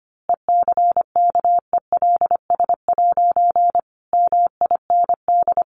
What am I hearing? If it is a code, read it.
Morse code, heard as a keyed tone, about 25 wpm, 700 Hz. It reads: ECKELS' MIND
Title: A Sound Of Thunder 25wpm